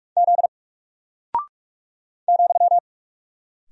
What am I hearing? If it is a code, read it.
Morse code, heard as a keyed tone, about 45 wpm, 700 Hz. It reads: Z ,
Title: Single Character (Letter:Number:Punctuation:Prosign) Alt 45wpm